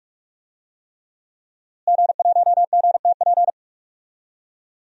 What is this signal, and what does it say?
Morse code, heard as a keyed tone, about 46 wpm, 700 Hz. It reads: G1GTP
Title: DXCC-100 Call Signs - 5 Characters - Repeated 3x - 45wpm